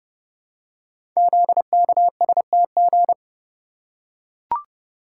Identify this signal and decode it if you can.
Morse code, heard as a keyed tone, about 28 wpm, 700 Hz. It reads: ZKSTG
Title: Code Groups - 5 Letters - 30wpm